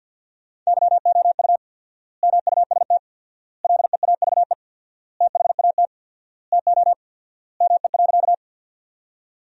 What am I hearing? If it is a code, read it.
Morse code, heard as a keyed tone, about 50 wpm, 700 Hz. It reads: YOU MUST LEAVE THAT TO ME.
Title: 4x Word-Spacing - Sentences from Top 400 Words - 50wpm